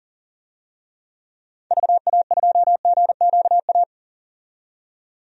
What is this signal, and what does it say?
Morse code, heard as a keyed tone, about 40 wpm, 700 Hz. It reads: VA2GQA
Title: DXCC-100 Call Signs - 6 Characters - 40wpm